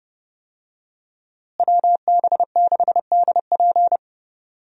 Morse code, heard as a keyed tone, about 31 wpm, 700 Hz. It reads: WB6DP
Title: USA Call Signs - 5 Characters - Repeated 3x - 30wpm